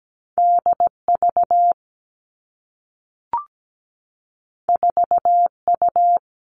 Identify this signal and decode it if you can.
Morse code, heard as a keyed tone, about 17 wpm, 700 Hz. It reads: D V 4 U
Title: Two Challenging Letter-Number Combinations 17wpm